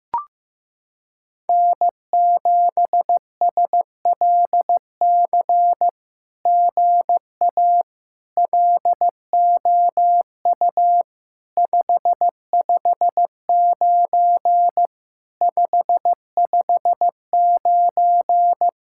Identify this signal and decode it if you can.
Morse code, heard as a keyed tone, about 15 wpm, 700 Hz. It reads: N7SLC GA LOU 559 559
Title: SOTA QSO Practice 15wpm